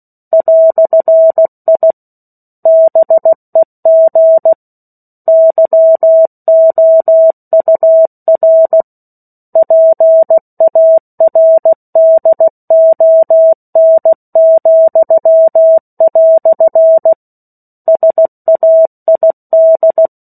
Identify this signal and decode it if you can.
Morse code, heard as a keyed tone, about 16 wpm, 650 Hz. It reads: "I BEG YOUR PARDON," SAID
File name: 03-Chapter-03 16 WPM